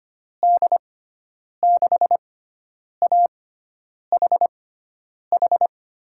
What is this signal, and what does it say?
Morse code, heard as a keyed tone, about 25 wpm, spaced as for 9 wpm, 700 Hz. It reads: D6AHH
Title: USA Call Signs - 6x Character Spacing - 25wpm